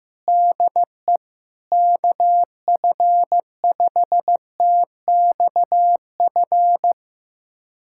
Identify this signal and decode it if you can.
Morse code, heard as a keyed tone, about 15 wpm, 700 Hz. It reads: DE KF5TXF
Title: USA Callsign DE Callsign 15wpm